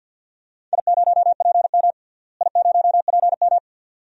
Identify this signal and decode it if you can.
Morse code, heard as a keyed tone, about 47 wpm, 700 Hz. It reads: I0PM I0PM
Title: DXCC-100 Call Signs - 4 Characters - Repeated 2x - 50wpm